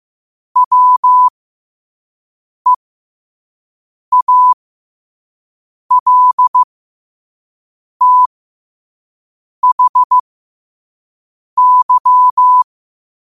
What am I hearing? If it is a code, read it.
Morse code, heard as a keyed tone, about 15 wpm, spaced as for 5 wpm, 1000 Hz. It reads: WEALTHY
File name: qotdfiles / 5wpm